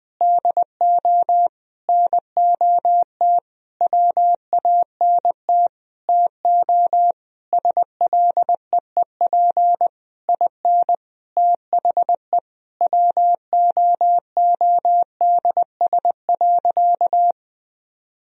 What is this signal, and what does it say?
Morse code, heard as a keyed tone, about 20 wpm, 700 Hz. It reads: DO NOT WANT TO SLEEP IN THE WOODS.
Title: Sentences from Top 1000 Words - Review Part 2 - 20wpm